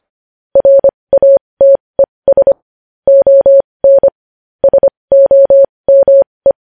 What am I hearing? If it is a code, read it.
Morse code, heard as a keyed tone, about 25 wpm, spaced as for 20 wpm, 550 Hz. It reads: RATES ON SOME
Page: news headlines in morse code at 20 wpm